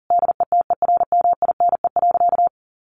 Morse code, heard as a keyed tone, about 40 wpm, 700 Hz. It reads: DETERMINE.
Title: Sentences from Top 1500 Words 40wpm